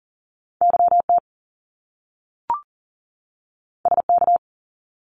Morse code, heard as a keyed tone, about 40 wpm, 700 Hz. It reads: YT SK
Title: Canadian Provinces and Territories 40wpm